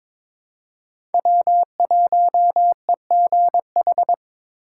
Morse code, heard as a keyed tone, about 22 wpm, 700 Hz. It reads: W1EGH
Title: USA Call Signs - 5 Characters - 4x Character Spacing - 22wpm